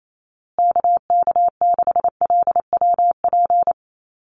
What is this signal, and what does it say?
Morse code, heard as a keyed tone, about 28 wpm, 700 Hz. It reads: KK6LWP
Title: USA Call Signs Repeated 2x 28wpm